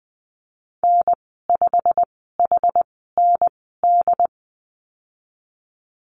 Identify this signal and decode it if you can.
Morse code, heard as a keyed tone, about 20 wpm, spaced as for 15 wpm, 700 Hz. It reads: N5HND
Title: 088 Lesson 15g - Number 5 - Call Signs 20wpm